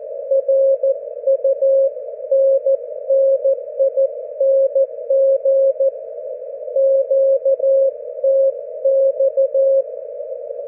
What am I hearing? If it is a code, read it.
Morse code, heard as a keyed tone, about 14 wpm, 540 Hz. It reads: RUNNING QTX
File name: GW3UEP / GW4HXO 40mWRF 42mls 24Jly2012